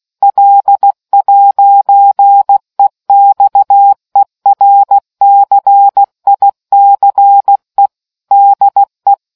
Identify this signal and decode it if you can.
Morse code, heard as a keyed tone, about 16 wpm, 800 Hz. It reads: L'EXERCICE DE